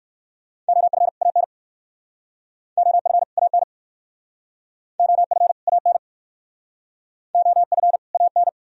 Morse code, heard as a keyed tone, about 59 wpm, 700 Hz. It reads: OF AN OF AN OF AN OF AN